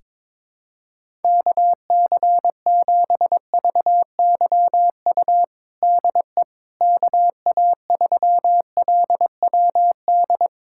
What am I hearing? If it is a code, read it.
Morse code, heard as a keyed tone, about 22 wpm, 700 Hz. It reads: KC7VYU DE KA3LWD
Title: USA Callsign DE Callsign 22wpm